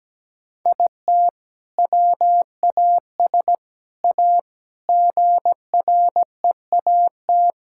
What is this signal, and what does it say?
Morse code, heard as a keyed tone, about 17 wpm, 700 Hz. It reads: IT WAS A GREAT
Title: Sentences from Top 1000 Words - Review Part 5 - 17wpm